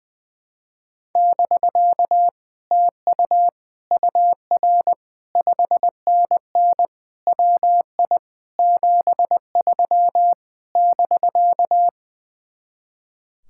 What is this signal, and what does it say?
Morse code, heard as a keyed tone, about 20 wpm, 700 Hz. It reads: <BK> TU UR 5NN WI 73 <BK>
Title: Speed-Racing Common QSO Phrases - 20wpm